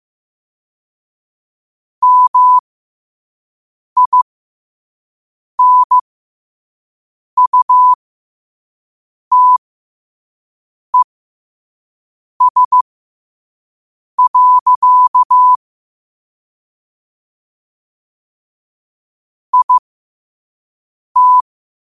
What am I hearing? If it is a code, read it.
Morse code, heard as a keyed tone, about 15 wpm, spaced as for 5 wpm, 1000 Hz. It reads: MINUTES. IT
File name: qotdfiles / 5wpm